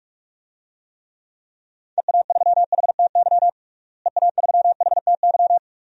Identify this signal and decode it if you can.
Morse code, heard as a keyed tone, about 41 wpm, 700 Hz. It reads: EA3HTY EA3HTY
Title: DXCC-100 Call Signs - 6 Characters - Repeated 2x - 45wpm